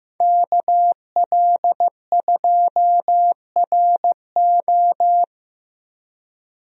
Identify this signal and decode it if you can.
Morse code, heard as a keyed tone, about 15 wpm, 700 Hz. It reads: KL2RO
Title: USA Call Signs - 5 Characters - Repeated 3x - 15wpm